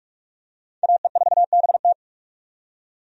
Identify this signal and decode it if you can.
Morse code, heard as a keyed tone, about 46 wpm, 700 Hz. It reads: AE4BT
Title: USA Call Signs - 5 Characters - 45wpm